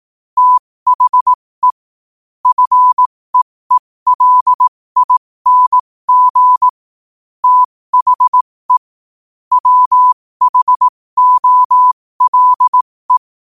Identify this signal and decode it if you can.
Morse code, heard as a keyed tone, about 18 wpm, spaced as for 15 wpm, 1000 Hz. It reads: THE FEELING THE WHOLE
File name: qotdfiles / 15wpm